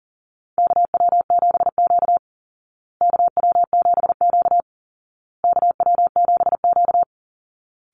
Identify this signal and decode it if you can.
Morse code, heard as a keyed tone, about 40 wpm, 700 Hz. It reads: KW7Q KW7Q KW7Q
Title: Call Signs Repeated 2x - 40wpm